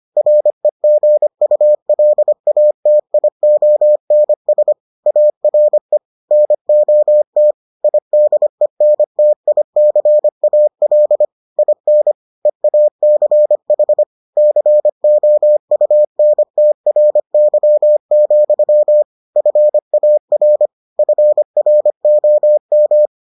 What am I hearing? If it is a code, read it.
Morse code, heard as a keyed tone, about 25 wpm, 600 Hz. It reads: REGULATIONS ARE NOT IDENTICAL IN EACH COUNTRY, FAR FROM